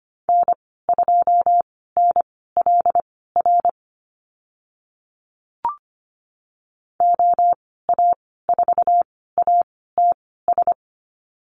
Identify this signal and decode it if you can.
Morse code, heard as a keyed tone, about 25 wpm, spaced as for 16 wpm, 700 Hz. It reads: N2NLR OA4ATS
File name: Lesson 15g - Number 5 - Call Signs 25wpm